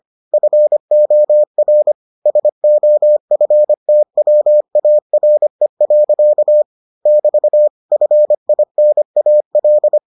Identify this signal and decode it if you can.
Morse code, heard as a keyed tone, about 25 wpm, 600 Hz. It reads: FOR SOFTWARE. <BT> FINAL